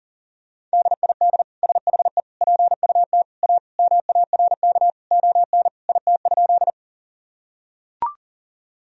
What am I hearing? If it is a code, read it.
Morse code, heard as a keyed tone, about 37 wpm, 700 Hz. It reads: DID SHE PUT A MARK ON IT?
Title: Sentences from Top 1000 Words - Review Part 1 - 40wpm